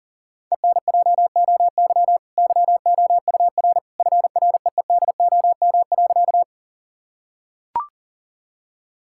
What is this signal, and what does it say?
Morse code, heard as a keyed tone, about 40 wpm, 700 Hz. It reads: ENJOY YOUR FREEDOM.
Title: Sentences from Top 1000 Words - Review Part 5 - 40wpm